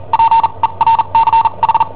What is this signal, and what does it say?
Morse code, heard as a keyed tone, about 43 wpm, spaced as for 33 wpm, 910 Hz. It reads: PERCH
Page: Cassetta esercizi preparata da I7OHP per IW5ELP